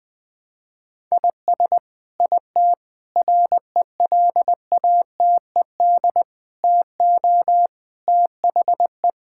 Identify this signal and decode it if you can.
Morse code, heard as a keyed tone, about 20 wpm, 700 Hz. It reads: IS IT RELATED TO THE
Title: Sentences from Top 1000 Words - Review Part 5 - 20wpm